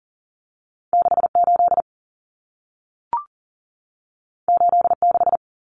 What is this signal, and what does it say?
Morse code, heard as a keyed tone, about 40 wpm, 700 Hz. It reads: 68 86
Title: Common Cut Numbers - 2-Digit 40wpm